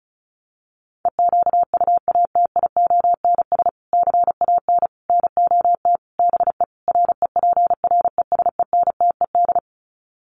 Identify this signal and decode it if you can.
Morse code, heard as a keyed tone, about 34 wpm, 700 Hz. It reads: EQUATIONS CAN NOT BE REPRESENTED
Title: Sets of 5 Words 35wpm